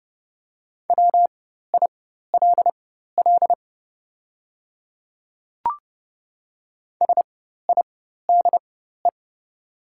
Morse code, heard as a keyed tone, about 30 wpm, spaced as for 14 wpm, 700 Hz. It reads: WILL SIDE
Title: 4x Character Spacing - Top 200 Words - 30wpm